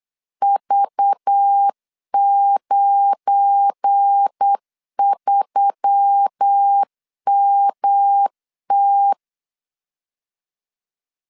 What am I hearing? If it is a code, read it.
Morse code, heard as a keyed tone, about 8 wpm, 800 Hz. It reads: V93MT
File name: LEZIONI / 8 YJ45 / 8 YJ 8wpm 3313